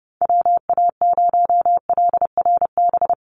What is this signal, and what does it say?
Morse code, heard as a keyed tone, about 30 wpm, 700 Hz. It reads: WA0LRB
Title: USA Call Signs 30wpm